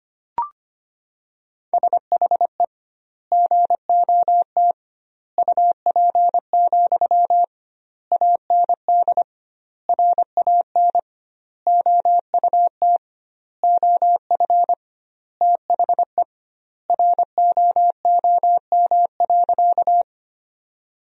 Morse code, heard as a keyed tone, about 25 wpm, 700 Hz. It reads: SHE GOT UP, AND RAN OUT OF THE ROOM.
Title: 2x Word-Spacing - Sentences from Top 500 Words - 25wpm